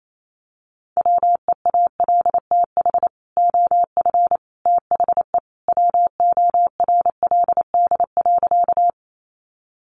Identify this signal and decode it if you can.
Morse code, heard as a keyed tone, about 28 wpm, 700 Hz. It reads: WEALTH OF THE WORLD.
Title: Sentences from Top 1200 Words 28wpm